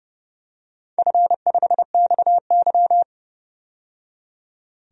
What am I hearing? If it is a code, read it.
Morse code, heard as a keyed tone, about 28 wpm, 700 Hz. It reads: F5XY
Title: DXCC-100 Call Signs - 4 Characters - 30wpm